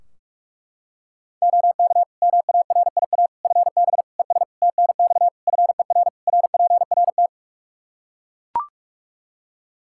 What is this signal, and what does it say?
Morse code, heard as a keyed tone, about 41 wpm, 700 Hz. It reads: OK MARIA FB ES TNX FER RPRT
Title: Practice QSO - Exchange 2 - Info - 45wpm